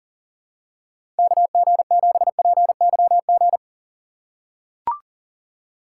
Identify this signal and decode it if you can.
Morse code, heard as a keyed tone, about 40 wpm, 700 Hz. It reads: KG7PYG
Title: USA Call Signs - 3x Character Spacing - 40wpm